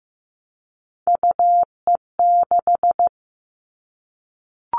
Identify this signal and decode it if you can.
Morse code, heard as a keyed tone, about 15 wpm, 700 Hz. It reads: UE6
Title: All Cut Numbers - 3-Digit 15wpm